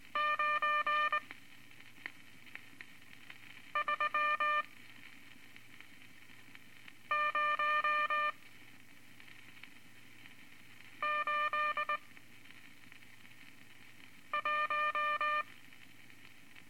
Morse code, heard as a keyed tone, about 21 wpm, spaced as for 3 wpm, 1200 Hz. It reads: 93081
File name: Morse Code Records / 14B LCM 157 Units V Groups of Five